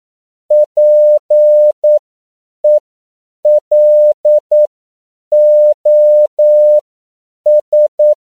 Morse code, heard as a keyed tone, about 9 wpm, spaced as for 7 wpm, 600 Hz. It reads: PELOS
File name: CW / 10PPM